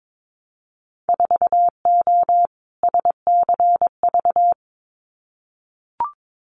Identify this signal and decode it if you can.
Morse code, heard as a keyed tone, about 22 wpm, 700 Hz. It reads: 4O SCV
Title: Winter Field Day Exchanges 22wpm